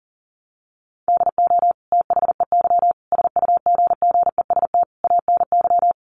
Morse code, heard as a keyed tone, about 37 wpm, 700 Hz. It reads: DO THEY SUGGEST ANY
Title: Sentences from Top 1300 Words 40wpm